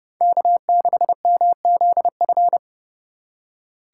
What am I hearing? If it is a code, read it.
Morse code, heard as a keyed tone, about 28 wpm, 700 Hz. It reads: K6MZF